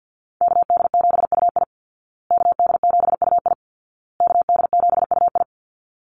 Morse code, heard as a keyed tone, about 50 wpm, 700 Hz. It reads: KD7UI KD7UI KD7UI
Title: Call Signs Repeated 2x - 50wpm